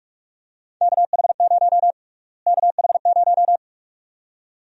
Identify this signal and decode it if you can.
Morse code, heard as a keyed tone, about 46 wpm, 700 Hz. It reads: KH0 KH0
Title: USA Call Sign Prefixes - Repeated 2x - 45wpm